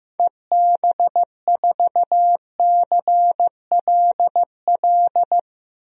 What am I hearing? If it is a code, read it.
Morse code, heard as a keyed tone, about 16 wpm, 700 Hz. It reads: EB4CLL
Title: DXCC-100 Call Signs - 6 Characters - Repeated 3x - 15wpm